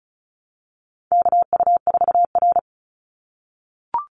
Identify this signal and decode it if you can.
Morse code, heard as a keyed tone, about 34 wpm, 700 Hz. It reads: KU4R
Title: NA SOTA Chaser Callsigns 35wpm